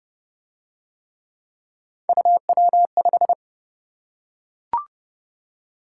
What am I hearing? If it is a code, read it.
Morse code, heard as a keyed tone, about 25 wpm, 700 Hz. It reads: UW5